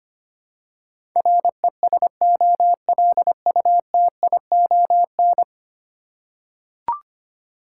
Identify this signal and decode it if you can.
Morse code, heard as a keyed tone, about 25 wpm, 700 Hz. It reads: RESOLUTION
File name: Ten Letter Words - 25wpm